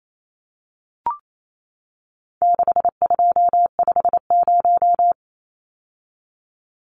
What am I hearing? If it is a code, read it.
Morse code, heard as a keyed tone, about 28 wpm, 700 Hz. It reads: 6250
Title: Numbers - 4-Digit 28wpm